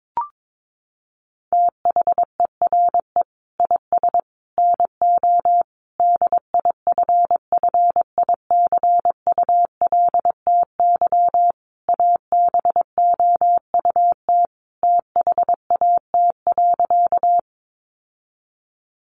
Morse code, heard as a keyed tone, about 22 wpm, 700 Hz. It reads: THERE IS NO DIFFICULTY ABOUT THAT.
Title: Sentences from Top 1500 Words 22wpm